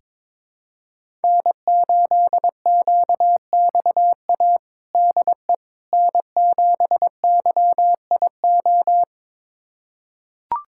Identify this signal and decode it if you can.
Morse code, heard as a keyed tone, about 21 wpm, 700 Hz. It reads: N8QXA DE N7YIO
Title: USA Callsign DE Callsign - 5 Characters - 22wpm